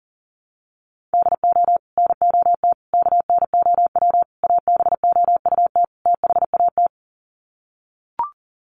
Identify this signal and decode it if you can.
Morse code, heard as a keyed tone, about 40 wpm, 700 Hz. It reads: DO NOT KNOW ABOUT THAT
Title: Sets of 5 Words Encore2 40wpm